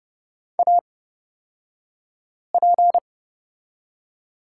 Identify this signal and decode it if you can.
Morse code, heard as a keyed tone, about 30 wpm, spaced as for 5 wpm, 700 Hz. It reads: AP
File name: Lesson 37d - Letter x - Two Characters 30wpm